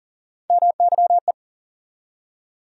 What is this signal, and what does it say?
Morse code, heard as a keyed tone, about 40 wpm, 700 Hz. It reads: MYE
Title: USA Call Sign Postfixes - 3 Letters 40wpm